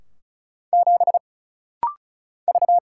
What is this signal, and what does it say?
Morse code, heard as a keyed tone, about 36 wpm, 700 Hz. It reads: 7 V
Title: Single Letter-Number - Rapid-Fire 35wpm